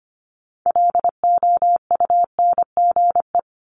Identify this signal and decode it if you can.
Morse code, heard as a keyed tone, about 25 wpm, 700 Hz. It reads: LOUNGE
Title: Multi-letter Phonemes 25wpm